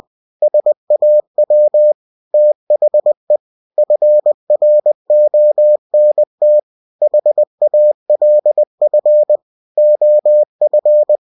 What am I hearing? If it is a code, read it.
Morse code, heard as a keyed tone, about 20 wpm, 600 Hz. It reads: SAW THE FRONT HALF OF
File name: cw-books / winnie-the-pooh / winnie-20wpm-page-0017